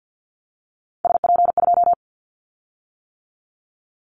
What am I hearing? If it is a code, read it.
Morse code, heard as a keyed tone, about 50 wpm, 700 Hz. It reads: SP2
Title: Callsign Prefixes for Top 25 DXCC - 50wpm